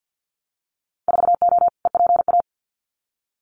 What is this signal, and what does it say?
Morse code, heard as a keyed tone, about 50 wpm, 700 Hz. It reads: 4O EPA
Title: Winter Field Day Exchanges 50wpm